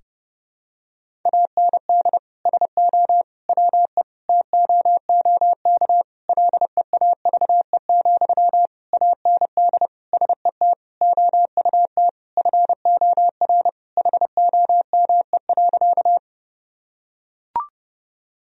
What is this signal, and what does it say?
Morse code, heard as a keyed tone, about 30 wpm, 700 Hz. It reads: AND SO WE TOOK LEAVE, AND SET OUT FOR HOME.
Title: Sentences from Top 1000 Words - Review Part 1 - 30wpm